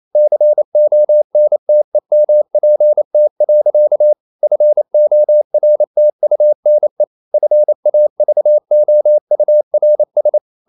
Morse code, heard as a keyed tone, about 28 wpm, 600 Hz. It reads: CONTEMPT. FORTUNE FAVOURS
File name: CW / Practice / EnglishSayings28